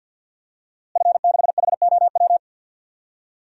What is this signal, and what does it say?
Morse code, heard as a keyed tone, about 47 wpm, 700 Hz. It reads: F6HOW
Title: DXCC-100 Call Signs - 5 Characters - Repeated 2x - 50wpm